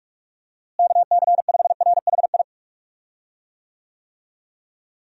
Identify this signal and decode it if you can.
Morse code, heard as a keyed tone, about 46 wpm, 700 Hz. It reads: KC5RHI
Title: USA Call Signs - 6 Characters - Repeated 3x - 45wpm